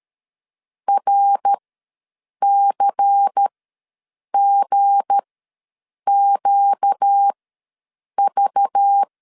Morse code, heard as a keyed tone, about 13 wpm, spaced as for 7 wpm, 800 Hz. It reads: RCGQV
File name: LEZIONI / 7 GF9Z / 7 G ws 12wpm 3140